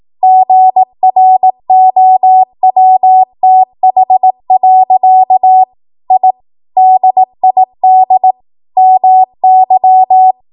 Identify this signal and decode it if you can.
Morse code, heard as a keyed tone, about 18 wpm, 750 Hz. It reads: GROWTH. I DID MY
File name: Archive / 18 WPM / 18WPM